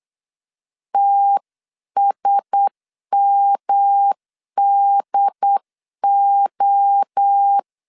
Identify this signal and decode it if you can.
Morse code, heard as a keyed tone, about 9 wpm, 800 Hz. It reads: TSMDO
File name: CW / DET / 1 6 1500C S C 8wpm 5219